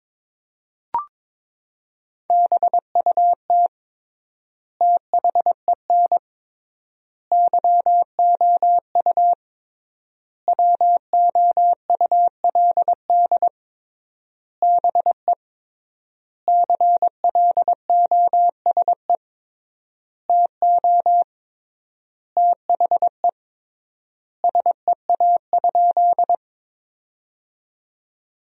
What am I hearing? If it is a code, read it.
Morse code, heard as a keyed tone, about 22 wpm, 700 Hz. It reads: BUT THEN YOU WOULD BE CLOSE TO THE SEA?
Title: 3x Word-Spacing - Sentences from Top 300 Words - 22wpm